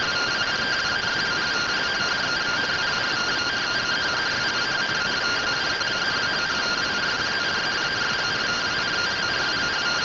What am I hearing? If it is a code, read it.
Two-tone teleprinter signal, about 170 Hz shift.